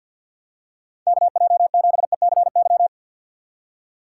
Morse code, heard as a keyed tone, about 47 wpm, 700 Hz. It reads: KJ7ECY
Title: USA Call Signs - 6 Characters - 6x Character Spacing - 50wpm